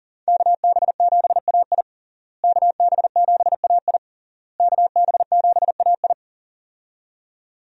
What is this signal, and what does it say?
Morse code, heard as a keyed tone, about 41 wpm, 700 Hz. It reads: KB7AI KB7AI KB7AI
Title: USA Call Signs - 5 Characters - Repeated 3x - 40wpm